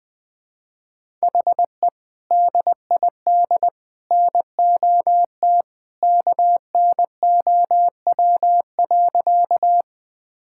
Morse code, heard as a keyed tone, about 20 wpm, 700 Hz. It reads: HE DID NOT KNOW.
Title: Sentences with 2-Word Phrases - 20wpm